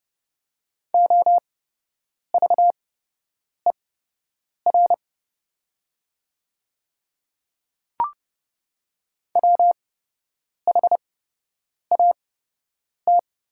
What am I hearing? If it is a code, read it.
Morse code, heard as a keyed tone, about 30 wpm, spaced as for 8 wpm, 700 Hz. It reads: OVER WHAT
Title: 8x Character Spacing - Top 200 Words - 30wpm